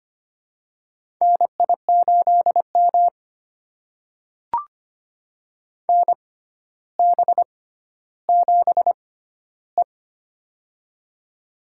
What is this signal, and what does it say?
Morse code, heard as a keyed tone, about 25 wpm, spaced as for 13 wpm, 700 Hz. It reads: NI8M NB7E
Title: USA Call Signs - 4 Characters - 6x Character Spacing - 25wpm